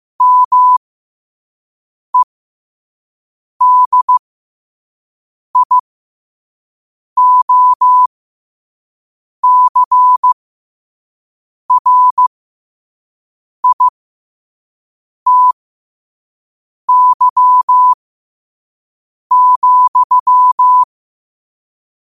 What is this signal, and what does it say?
Morse code, heard as a keyed tone, about 15 wpm, spaced as for 5 wpm, 1000 Hz. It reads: MEDIOCRITY,
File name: qotdfiles / 5wpm